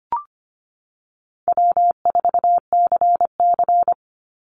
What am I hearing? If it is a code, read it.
Morse code, heard as a keyed tone, about 25 wpm, 700 Hz. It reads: W4CC
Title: Easy Call Signs Repeated 2x - 25wpm